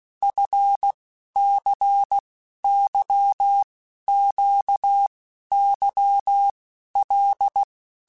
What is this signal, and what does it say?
Morse code, heard as a keyed tone, about 16 wpm, 800 Hz. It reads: F C Y Q Y L
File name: Basic Letter Groups By7Groups / LFCYQ / LFCYQ@15